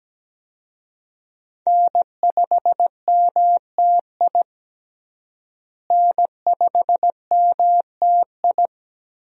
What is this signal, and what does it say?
Morse code, heard as a keyed tone, about 17 wpm, 700 Hz. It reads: N5MTI N5MTI
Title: USA Call Signs - 5 Characters - Repeated 3x - 17wpm